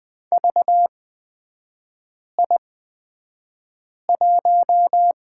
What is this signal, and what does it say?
Morse code, heard as a keyed tone, about 20 wpm, spaced as for 5 wpm, 700 Hz. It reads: VI1
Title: 255 Lesson 41e - Period - Three Characters 20wpm